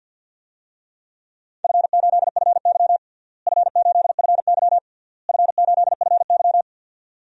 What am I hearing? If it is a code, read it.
Morse code, heard as a keyed tone, about 51 wpm, 700 Hz. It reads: F8FY F8FY F8FY
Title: DXCC-100 Call Signs - 4 Characters - Repeated 3x - 50wpm